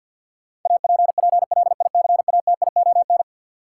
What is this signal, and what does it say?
Morse code, heard as a keyed tone, about 47 wpm, 700 Hz. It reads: APPLICATION
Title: Sentences from Top 1200 Words - New Word Drill Alt - 50wpm